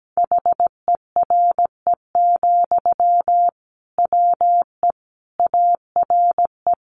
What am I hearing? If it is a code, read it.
Morse code, heard as a keyed tone, about 17 wpm, 700 Hz. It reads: HERE, WE ARE
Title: Sentences from Top 1400 Words 17wpm